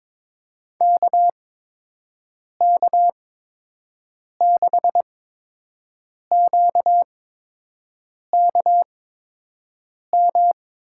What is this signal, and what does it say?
Morse code, heard as a keyed tone, about 22 wpm, spaced as for 6 wpm, 700 Hz. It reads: KK6QKM
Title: USA Call Signs - 6 Characters - 8x Character Spacing - 22wpm